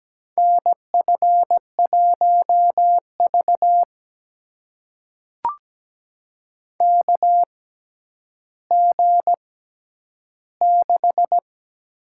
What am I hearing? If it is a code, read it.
Morse code, heard as a keyed tone, about 17 wpm, spaced as for 10 wpm, 700 Hz. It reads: NF1V KG6
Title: USA Call Signs - 6x Character Spacing - 17wpm